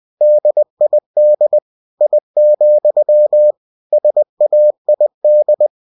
Morse code, heard as a keyed tone, about 20 wpm, 600 Hz. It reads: DID I, SAID
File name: cw-books / winnie-the-pooh / winnie-20wpm-page-0000